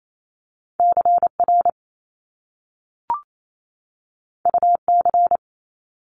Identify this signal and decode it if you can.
Morse code, heard as a keyed tone, about 28 wpm, 700 Hz. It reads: CR UC
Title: Top Two Letter Combinations in Words 28wpm